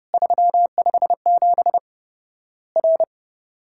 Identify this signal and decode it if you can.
Morse code, heard as a keyed tone, about 30 wpm, 700 Hz. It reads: 357 R
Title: Worked All Europe DX Contest - QTC Exchange - 30wpm